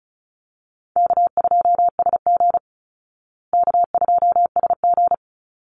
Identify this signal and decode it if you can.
Morse code, heard as a keyed tone, about 34 wpm, 700 Hz. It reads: K2SG K2SG
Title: Easy Call Signs Repeated 3x - 35wpm